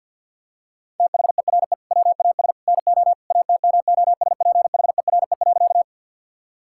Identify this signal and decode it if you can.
Morse code, heard as a keyed tone, about 50 wpm, 700 Hz. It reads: THERE WAS NO ATMOSPHERE.
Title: Sentences from Top 1600 Words 50wpm